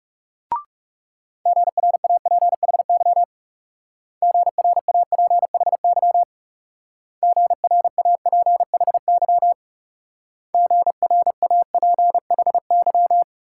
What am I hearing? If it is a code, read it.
Morse code, heard as a keyed tone, about 37 wpm, 700 Hz. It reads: GRAPHY GRAPHY GRAPHY GRAPHY
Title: Speed-Racing Multi-letter Phonemes - 30wpm